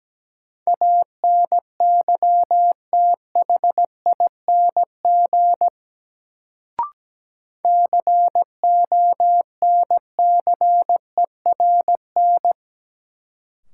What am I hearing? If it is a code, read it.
Morse code, heard as a keyed tone, about 17 wpm, 700 Hz. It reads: ANYTHING CONCERN
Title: Sentences from Top 1000 Words - New Word Drill - 17wpm